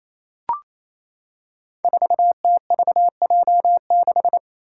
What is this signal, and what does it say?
Morse code, heard as a keyed tone, about 28 wpm, 700 Hz. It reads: 4TVJ6
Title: Five Challenging Letter-Number Combinations 28wpm